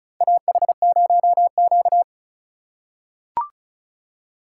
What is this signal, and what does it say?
Morse code, heard as a keyed tone, about 36 wpm, 700 Hz. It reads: AH0Q
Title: USA Call Signs - 4x Character Spacing - 35wpm